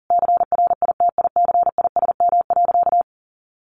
Morse code, heard as a keyed tone, about 40 wpm, 700 Hz. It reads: CRITICISM.
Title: Sentences from Top 1300 Words 40wpm